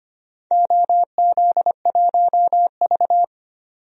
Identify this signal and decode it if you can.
Morse code, heard as a keyed tone, about 25 wpm, 700 Hz. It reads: OZ1V
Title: DXCC-100 Call Signs - 4 Characters - 25wpm